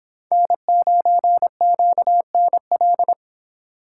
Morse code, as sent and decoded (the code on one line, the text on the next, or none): -. ----. --.- -. .-..
N9QNL